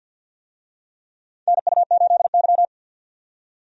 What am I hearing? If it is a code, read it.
Morse code, heard as a keyed tone, about 52 wpm, 700 Hz. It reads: NU8Y
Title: USA Call Signs - 4 Characters - 50wpm